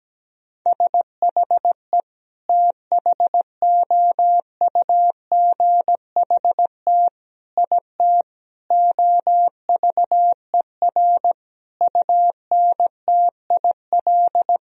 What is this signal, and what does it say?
Morse code, heard as a keyed tone, about 17 wpm, 700 Hz. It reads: SHE THOUGHT IT OVER UNTIL
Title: Sentences from Top 1000 Words - Review Part 1 - 17wpm